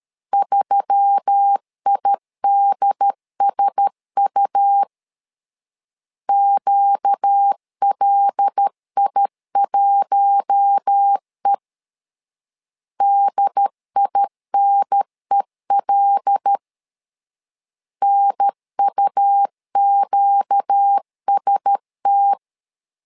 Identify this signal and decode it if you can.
Morse code, heard as a keyed tone, about 13 wpm, 800 Hz. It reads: 3IDSU QLI1E DINEL NUQST
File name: LEZIONI / 5 LBVC / 5 L 12wpm 1913